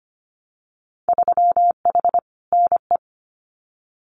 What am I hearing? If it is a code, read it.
Morse code, heard as a keyed tone, about 25 wpm, 690 Hz. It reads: 3H NE
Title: Winter Field Day Exchanges Encore 25wpm